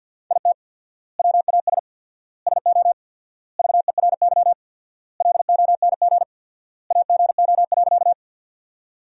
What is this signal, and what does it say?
Morse code, heard as a keyed tone, about 50 wpm, 700 Hz. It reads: IT WAS SO VERY LONG AGO.
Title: 4x Word-Spacing - Sentences from Top 500 Words - 50wpm